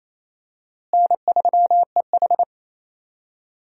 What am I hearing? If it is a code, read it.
Morse code, heard as a keyed tone, about 27 wpm, 700 Hz. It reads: N3EH
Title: USA Call Signs - 4 Characters - Repeated 3x - 28wpm